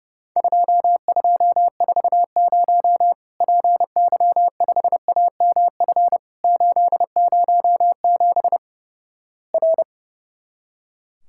Morse code, heard as a keyed tone, about 30 wpm, 700 Hz. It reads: 2240 PY5AMF 807 R
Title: Worked All Europe DX Contest - QTC Exchange - 30wpm